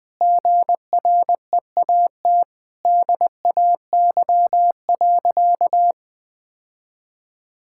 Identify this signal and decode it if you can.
Morse code, heard as a keyed tone, about 21 wpm, 700 Hz. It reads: GREAT DAY.
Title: Sentences with 6-Word Phrases - 20wpm